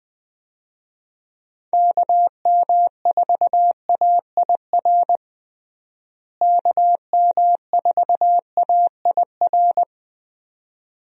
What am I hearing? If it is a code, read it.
Morse code, heard as a keyed tone, about 20 wpm, 700 Hz. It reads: KM4AIR KM4AIR
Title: USA Call Signs - 6 Characters - Repeated 2x - 20wpm